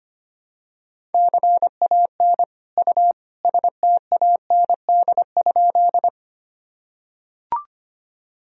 Morse code, heard as a keyed tone, about 25 wpm, 700 Hz. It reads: CAN U STAND?
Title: Sentences from Top 300 Words and Abbreviations 25wpm